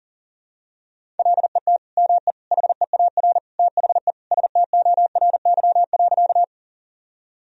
Morse code, heard as a keyed tone, about 40 wpm, 700 Hz. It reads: LET ME HEAR THE STORY.
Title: Sentences from Top 1000 Words - Review Part 1 - 40wpm